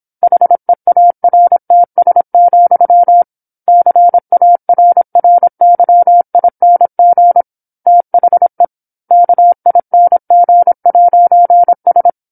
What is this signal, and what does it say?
Morse code, heard as a keyed tone, about 27 wpm, 700 Hz. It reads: HEARTS, CARRYING THE KING'S
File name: 08-Chapter-08 26 WPM